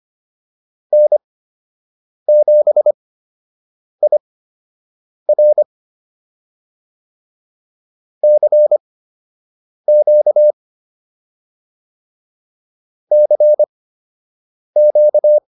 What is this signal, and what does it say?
Morse code, heard as a keyed tone, about 25 wpm, spaced as for 7 wpm, 600 Hz. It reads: N7IR CQ CQ